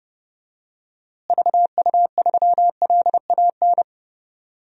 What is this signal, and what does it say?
Morse code, heard as a keyed tone, about 30 wpm, 700 Hz. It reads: VU3LAN
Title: DXCC-100 Call Signs - 6 Characters - Repeated 3x - 30wpm